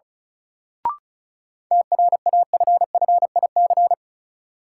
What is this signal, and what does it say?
Morse code, heard as a keyed tone, about 36 wpm, 700 Hz. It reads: TRAFFIC
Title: Sentences from Top 1300 Words - New Word Drill Alt - 35wpm